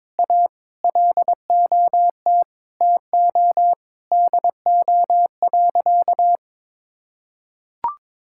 Morse code, heard as a keyed tone, about 22 wpm, 700 Hz. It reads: A LOT TO DO.
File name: practice / 22wpm / Sentences with 2-Word Phrases - 22wpm